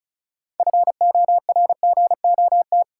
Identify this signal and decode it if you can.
Morse code, heard as a keyed tone, about 34 wpm, 700 Hz. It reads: FORGOT
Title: Sentences from Top 1400 Words - New Word Drill Alt - 35wpm